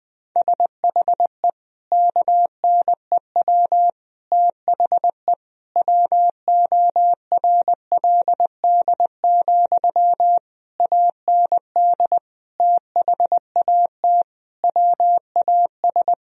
Morse code, heard as a keyed tone, about 21 wpm, 700 Hz. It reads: SHE KNEW THE WORLD, AND THAT WAS
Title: Sentences from Top 1000 Words - Review Part 4 - 20wpm